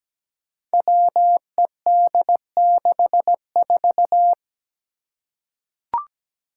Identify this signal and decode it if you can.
Morse code, heard as a keyed tone, about 17 wpm, 700 Hz. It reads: WED64
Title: Five Challenging Letter-Number Combinations 17wpm